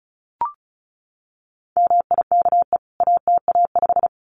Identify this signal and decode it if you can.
Morse code, heard as a keyed tone, about 34 wpm, 700 Hz. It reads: MIKE ATA5
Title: CWT Exchange 35wpm